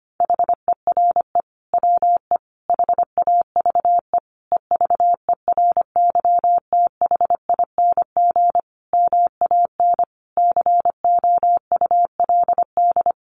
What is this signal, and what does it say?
Morse code, heard as a keyed tone, about 25 wpm, 700 Hz. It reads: HERE WE HAVE EVERYTHING MAN COULD
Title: Sentences from Top 1100 Words 25wpm